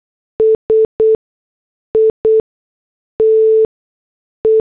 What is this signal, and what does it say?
Morse code, heard as a keyed tone, about 8 wpm, spaced as for 6 wpm, 440 Hz. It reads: SITE
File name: morse / headlines